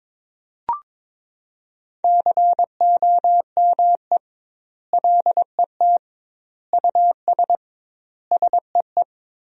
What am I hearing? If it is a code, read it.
Morse code, heard as a keyed tone, about 22 wpm, 700 Hz. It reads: COME LET US SEE
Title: 2x Word-Spacing - Sentences from Top 300 Words - 22wpm